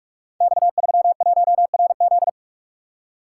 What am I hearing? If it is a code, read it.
Morse code, heard as a keyed tone, about 44 wpm, 690 Hz. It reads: X31RZ
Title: Code Groups - 5 Letter:Numbers - 45wpm